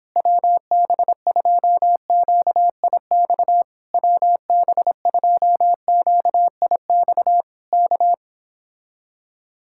Morse code, heard as a keyed tone, about 26 wpm, 700 Hz. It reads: WB2QIX WB2QIX K
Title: Speed-Racing Common QSO Phrases Encore - 22wpm